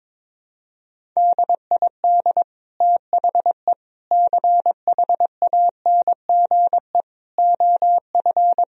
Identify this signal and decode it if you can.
Morse code, heard as a keyed tone, about 22 wpm, 700 Hz. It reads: DID THE CHANGE OF
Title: Sentences from Top 1000 Words - Review Part 4 - 22wpm